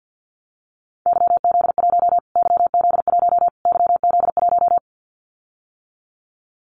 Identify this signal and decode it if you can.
Morse code, heard as a keyed tone, about 50 wpm, 700 Hz. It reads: YZ1 YZ1 YZ1
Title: Callsign Prefixes for Top 25 DXCC - Repeated 3x - 50wpm